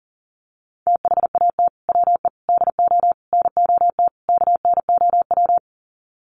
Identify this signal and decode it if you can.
Morse code, heard as a keyed tone, about 40 wpm, 700 Hz. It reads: THAT WE DO NOT KNOW
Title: Sets of 5 Words Encore2 40wpm